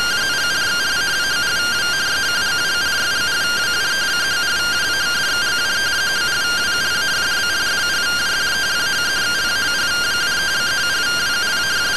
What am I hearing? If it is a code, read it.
Two-tone teleprinter signal, about 170 Hz shift.
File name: findings / MR CORONAL fr1 -131.gen-16 rtty rx